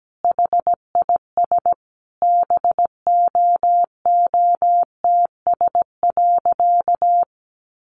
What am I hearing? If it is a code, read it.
Morse code, heard as a keyed tone, about 17 wpm, 700 Hz. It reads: HIS BOOTS.
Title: A Sound Of Thunder 17wpm